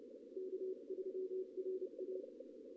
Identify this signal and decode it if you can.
Morse code, heard as a keyed tone, about 29 wpm, 380 Hz. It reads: K3RA
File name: contests / ss-ssb-k5tr- -1625